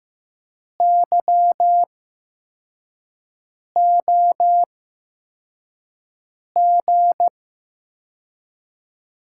Morse code, heard as a keyed tone, about 15 wpm, spaced as for 4 wpm, 700 Hz. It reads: YOG
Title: USA Call Signs - 8x Character Spacing - 15wpm